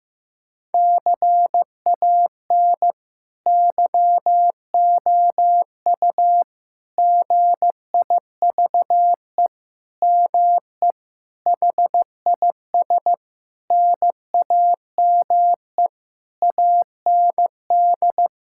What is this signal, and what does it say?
Morse code, heard as a keyed tone, about 15 wpm, 700 Hz. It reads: CAN YOU GIVE ME HIS NAME AND
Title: Sentences from Top 1000 Words - Review Part 6 - 15wpm